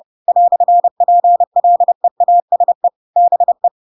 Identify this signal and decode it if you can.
Morse code, heard as a keyed tone, about 30 wpm, 700 Hz. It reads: "PLEASE BE
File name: Pinocchio / Pinocchio ch1 30 30 ik1hge